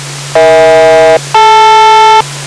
Aircraft SELCAL, heard as two pairs of tones. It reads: EH-DL